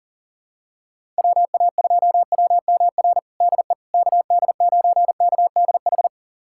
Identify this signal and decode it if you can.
Morse code, heard as a keyed tone, about 37 wpm, 700 Hz. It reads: WA2WMR DE KD9KDH
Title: USA Callsign DE Callsign - 6 Characters - 40wpm